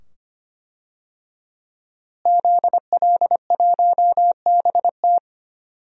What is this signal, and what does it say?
Morse code, heard as a keyed tone, about 25 wpm, 700 Hz. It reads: ZL1BT
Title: DXCC-100 Call Signs - 5 Characters - Repeated 2x - 25wpm